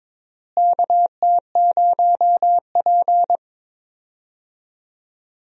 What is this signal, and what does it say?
Morse code, heard as a keyed tone, about 22 wpm, 700 Hz. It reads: KT0P
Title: USA Call Signs - 4 Characters - Repeated 3x - 22wpm